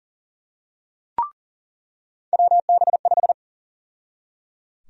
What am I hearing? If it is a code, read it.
Morse code, heard as a keyed tone, about 37 wpm, 700 Hz. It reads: WB5
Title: USA Call Sign Prefixes - 40wpm